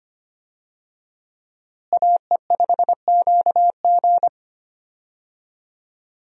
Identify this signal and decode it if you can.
Morse code, heard as a keyed tone, about 25 wpm, 700 Hz. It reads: AE5QG
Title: USA Call Signs - 5 Characters - Repeated 3x - 25wpm